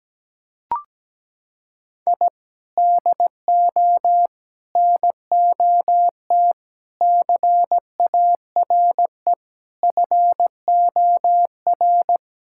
Morse code, heard as a keyed tone, about 17 wpm, 700 Hz. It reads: I DO NOT CARE FOR
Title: Sentences from Top 1000 Words - Review Part 6 - 17wpm